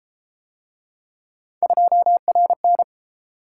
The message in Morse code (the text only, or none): 2RN